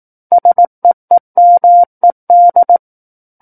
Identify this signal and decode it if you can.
Morse code, heard as a keyed tone, about 18 wpm, 700 Hz. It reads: SEEMED
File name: 18 WPM